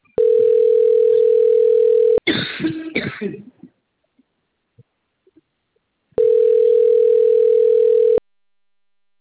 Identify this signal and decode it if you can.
Ringback tone (North America), 440 + 480 Hz.
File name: out- - - - - .154